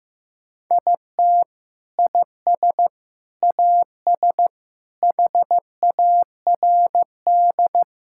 Morse code, heard as a keyed tone, about 15 wpm, 700 Hz. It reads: IT IS AS HARD
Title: Sentences from Top 1600 Words 15wpm